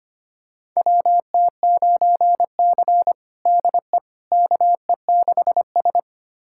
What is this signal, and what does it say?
Morse code, heard as a keyed tone, about 25 wpm, 700 Hz. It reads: WT9C DE KE6S
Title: USA Callsign DE Callsign - 4 Characters - 25wpm